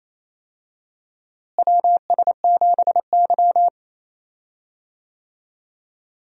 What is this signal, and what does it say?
Morse code, heard as a keyed tone, about 28 wpm, 700 Hz. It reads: WS7Y
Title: USA Call Signs - 4 Characters - Repeated 2x - 28wpm